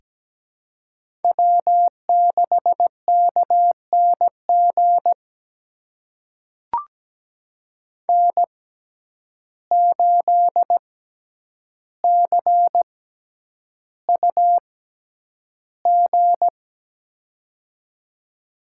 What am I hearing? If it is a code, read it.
Morse code, heard as a keyed tone, about 17 wpm, spaced as for 9 wpm, 700 Hz. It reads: W6KNG N8CUG
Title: USA Call Signs - 5 Characters - 6x Character Spacing - 17wpm